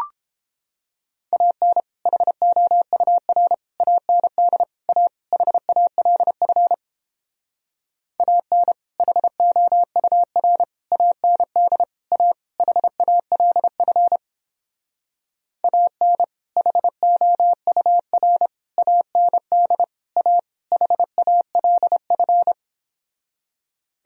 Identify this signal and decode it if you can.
Morse code, heard as a keyed tone, about 29 wpm, 700 Hz. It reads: AN HOUR AND A HALF AN HOUR AND A HALF AN HOUR AND A HALF